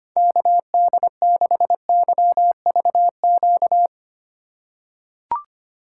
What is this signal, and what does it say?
Morse code, heard as a keyed tone, about 25 wpm, 700 Hz. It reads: KD6YVQ
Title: USA Call Signs - 6 Characters - 2x Character Spacing - 25wpm